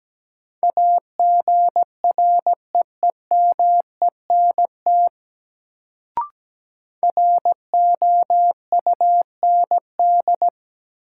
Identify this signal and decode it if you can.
Morse code, heard as a keyed tone, about 17 wpm, 700 Hz. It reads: AGREEMENT ROUND
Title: Sets of 5 Words Encore - New Word Drill - 17wpm